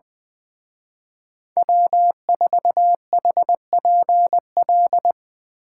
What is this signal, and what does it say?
Morse code, heard as a keyed tone, about 23 wpm, 700 Hz. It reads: W4HPL
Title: Speed Racing - USA 1x3 Call Signs 20wpm